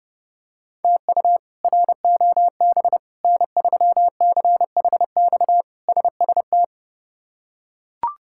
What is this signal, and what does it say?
Morse code, heard as a keyed tone, about 30 wpm, 700 Hz. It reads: TU ROB N3CHX SST
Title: K1USN SST QSO Practice - 30wpm